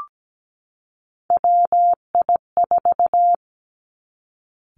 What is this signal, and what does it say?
Morse code, heard as a keyed tone, about 17 wpm, 700 Hz. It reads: WI4
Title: Three Challenging Letter-Number Combinations 17wpm